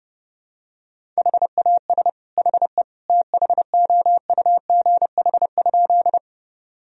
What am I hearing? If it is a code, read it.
Morse code, heard as a keyed tone, about 30 wpm, 700 Hz. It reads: HAS HE THOUGH?
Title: Sentences from Top 1000 Words - Review Part 2 - 30wpm